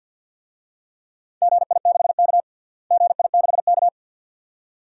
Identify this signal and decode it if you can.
Morse code, heard as a keyed tone, about 47 wpm, 700 Hz. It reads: GI6K GI6K
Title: DXCC-100 Call Signs - 4 Characters - Repeated 2x - 50wpm